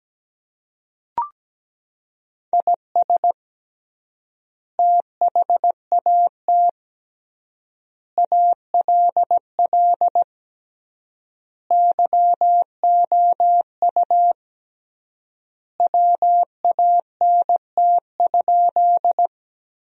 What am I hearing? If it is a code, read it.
Morse code, heard as a keyed tone, about 17 wpm, 700 Hz. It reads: IS THAT ALL YOU WANT?
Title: 3x Word-Spacing - Sentences from Top 200 Words - 17wpm